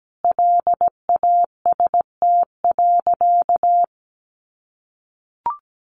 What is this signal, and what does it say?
Morse code, heard as a keyed tone, about 17 wpm, 700 Hz. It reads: LAST.
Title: Sentences from Top 300 Words 17wpm